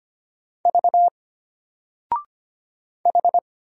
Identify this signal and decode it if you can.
Morse code, heard as a keyed tone, about 25 wpm, 700 Hz. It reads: V H
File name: Single-Letter-Number-25wpm